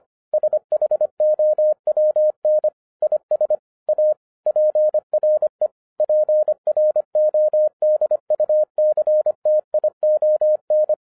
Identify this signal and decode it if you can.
Morse code, heard as a keyed tone, about 25 wpm, 610 Hz. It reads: SHOWN IS A PRE PRODUCTION
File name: 2011-10-mp3 / n14-MORSE EXPRESS 2011 CHRISTMAS KEY 25wpm